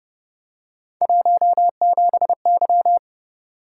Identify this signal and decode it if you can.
Morse code, heard as a keyed tone, about 31 wpm, 700 Hz. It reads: 17Y